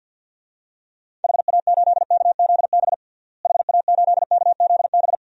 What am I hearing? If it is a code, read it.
Morse code, heard as a keyed tone, about 48 wpm, 700 Hz. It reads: HA8KZB HA8KZB
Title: DXCC-100 Call Signs - 6 Characters - Repeated 3x - 50wpm